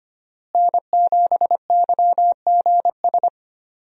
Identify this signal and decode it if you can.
Morse code, heard as a keyed tone, about 25 wpm, 700 Hz. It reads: N7YGS
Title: USA Call Signs - 5 Characters - Repeated 2x - 25wpm